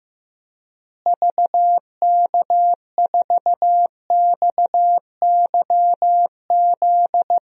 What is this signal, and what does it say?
Morse code, heard as a keyed tone, about 15 wpm, 700 Hz. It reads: VK4XYZ
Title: DXCC-100 Call Signs - 6 Characters - Repeated 3x - 15wpm